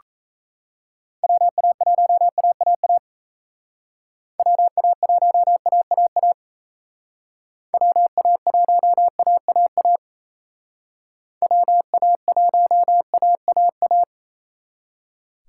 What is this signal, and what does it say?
Morse code, heard as a keyed tone, about 32 wpm, 700 Hz. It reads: WA1AAA WA1AAA WA1AAA WA1AAA
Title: Speed Racing - USA 2x3 Call Signs 28wpm